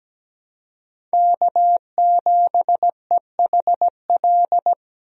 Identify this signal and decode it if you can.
Morse code, heard as a keyed tone, about 17 wpm, 700 Hz. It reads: K7EHL
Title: USA Call Signs - 5 Characters - 17wpm